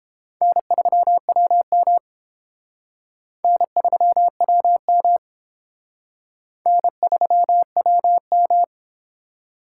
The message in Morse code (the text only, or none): N3WM N3WM N3WM